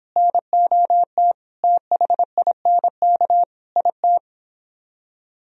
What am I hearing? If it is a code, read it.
Morse code, heard as a keyed tone, about 29 wpm, 700 Hz. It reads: NOT THINK IT